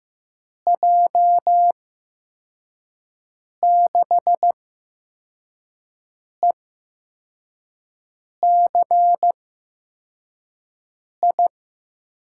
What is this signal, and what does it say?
Morse code, heard as a keyed tone, about 15 wpm, spaced as for 4 wpm, 700 Hz. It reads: J6ECI
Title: USA Call Signs - 8x Character Spacing - 15wpm